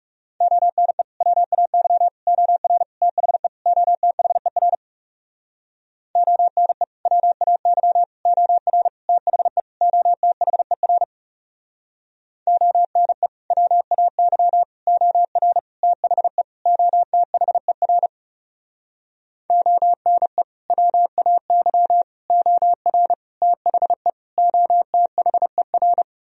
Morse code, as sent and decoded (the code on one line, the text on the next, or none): --- -. / .-- .- -.-- / --- .-. / - .... / --- - .... .-. / --- -. / .-- .- -.-- / --- .-. / - .... / --- - .... .-. / --- -. / .-- .- -.-- / --- .-. / - .... / --- - .... .-. / --- -. / .-- .- -.-- / --- .-. / - .... / --- - .... .-.
ONE WAY OR THE OTHER ONE WAY OR THE OTHER ONE WAY OR THE OTHER ONE WAY OR THE OTHER